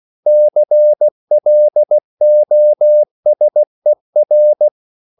Morse code, heard as a keyed tone, about 16 wpm, 600 Hz. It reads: CLOSER
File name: CW / Practice / EnglishQuotations0116